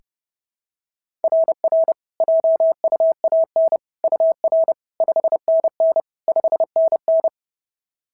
Morse code, heard as a keyed tone, about 32 wpm, 690 Hz. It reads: RR JUAN UR 5NN 5NN
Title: SOTA QSO Practice 30wpm